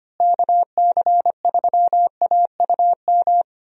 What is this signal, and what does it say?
Morse code, heard as a keyed tone, about 25 wpm, 700 Hz. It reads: KC3AUM
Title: USA Call Signs - 6 Characters - Repeated 2x - 25wpm